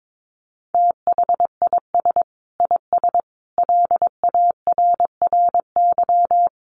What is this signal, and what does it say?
Morse code, heard as a keyed tone, about 22 wpm, 700 Hz. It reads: THIS IS LARRY
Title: Sets of 5 Words Encore2 22wpm